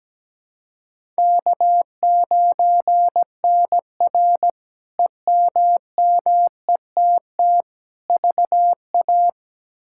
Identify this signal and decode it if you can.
Morse code, heard as a keyed tone, about 17 wpm, 700 Hz. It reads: K9NR EMMETT VA
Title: K1USN SST QSO Practice - 17wpm